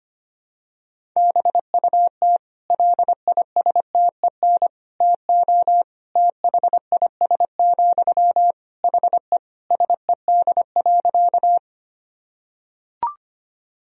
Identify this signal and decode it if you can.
Morse code, heard as a keyed tone, about 25 wpm, 700 Hz. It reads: BUT LISTEN TO THIS, HE SED.
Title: Sentences from Top 400 Words and Abbreviations 25wpm